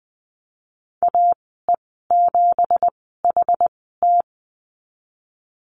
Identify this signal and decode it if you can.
Morse code, heard as a keyed tone, about 20 wpm, spaced as for 15 wpm, 700 Hz. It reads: AE7HT
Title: 167 Lesson 27g - Number 7 - Call Signs 20wpm